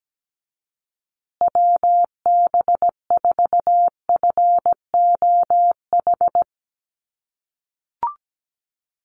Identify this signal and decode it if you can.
Morse code, heard as a keyed tone, about 17 wpm, 700 Hz. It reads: WB4FOH
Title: USA Call Signs Repeated 3x 17wpm